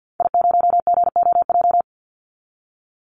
Morse code, heard as a keyed tone, about 50 wpm, 700 Hz. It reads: I0GOJ
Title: Call Signs Repeated 2x - 50wpm